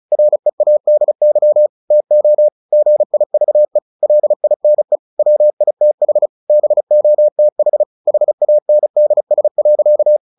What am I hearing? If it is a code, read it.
Morse code, heard as a keyed tone, about 34 wpm, 600 Hz. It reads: READY TO GIVE LINE WITH BOTH HANDS.
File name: the old man and the sea 2 35wpm